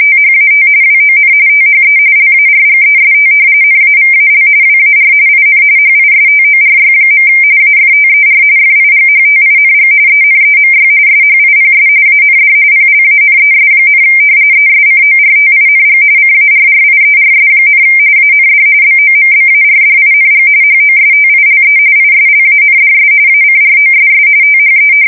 RTTY, about 45.45 baud. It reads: THAT WON THE LARGEST NAVAL WAR IN HISTORY. AFTER FIGHTING ENEMY SHIPS AND PLANES, ESCORTING CARRIERS, AND BOMBARDING HOSTILE ISLANDS ACROSS THE
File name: fsk170 / battleshipnj